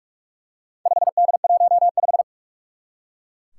Morse code, heard as a keyed tone, about 44 wpm, 700 Hz. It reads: 5D15
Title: Four Challenging Letter-Number Combinations 45wpm